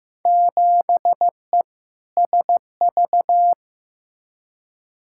Morse code, heard as a keyed tone, about 16 wpm, 700 Hz. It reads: 7E SV
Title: Field Day Exchanges Encore - 15wpm